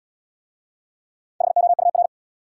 Morse code, heard as a keyed tone, about 67 wpm, 700 Hz. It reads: HOUR